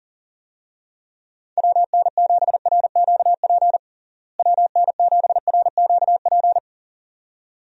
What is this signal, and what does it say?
Morse code, heard as a keyed tone, about 40 wpm, 700 Hz. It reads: WN7RQP WN7RQP
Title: USA Call Signs - 6 Characters - Repeated 2x - 40wpm